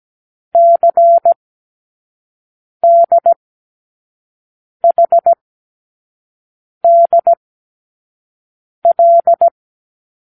Morse code, heard as a keyed tone, about 17 wpm, spaced as for 5 wpm, 680 Hz. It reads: CDHDL
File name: k7qo / Lesson18(A-L)